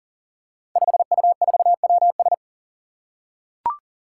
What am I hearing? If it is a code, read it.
Morse code, heard as a keyed tone, about 38 wpm, 700 Hz. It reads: 5U4WS
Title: Five Challenging Letter-Number Combinations 40wpm